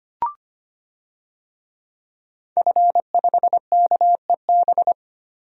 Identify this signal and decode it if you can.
Morse code, heard as a keyed tone, about 25 wpm, 700 Hz. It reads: F5KEB
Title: DXCC-100 Call Signs - 5 Characters - 25wpm